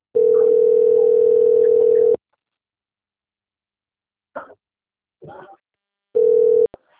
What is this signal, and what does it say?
Ringback tone (North America), 440 + 480 Hz.